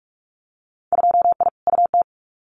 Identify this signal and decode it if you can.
Morse code, heard as a keyed tone, about 46 wpm, 700 Hz. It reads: 2I UT
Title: Winter Field Day Exchanges 45wpm